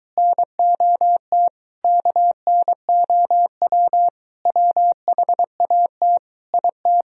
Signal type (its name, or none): Morse code